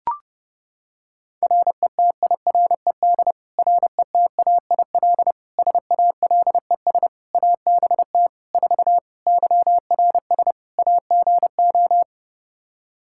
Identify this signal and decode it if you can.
Morse code, heard as a keyed tone, about 30 wpm, 700 Hz. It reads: RETIRED RETAIL SALES ABT 4 YRS AGO
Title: Practice QSO - Exchange 3 - More Info - 30wpm